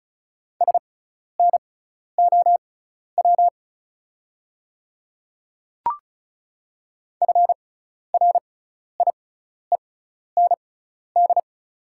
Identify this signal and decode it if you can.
Morse code, heard as a keyed tone, about 34 wpm, spaced as for 12 wpm, 690 Hz. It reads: SNOW FRIEND
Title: 6x Character Spacing - Top 500 Words - 35wpm